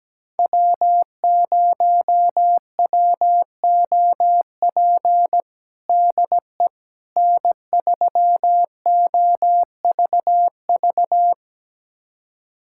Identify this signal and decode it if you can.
Morse code, heard as a keyed tone, about 17 wpm, 700 Hz. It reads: W0WOP DE N3OVV
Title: USA Callsign DE Callsign - 5 Characters - 17wpm